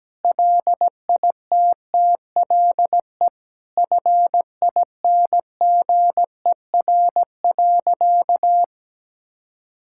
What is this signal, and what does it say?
Morse code, heard as a keyed tone, about 17 wpm, 700 Hz. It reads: LITTLE FINGER.
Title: Sentences from Top 1600 Words 17wpm